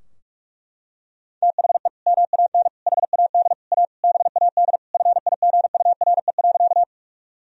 Morse code, heard as a keyed tone, about 41 wpm, 700 Hz. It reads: THE MAN HAD A BAD FIGURE.
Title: Sentences from Top 1000 Words - Review Part 2 - 45wpm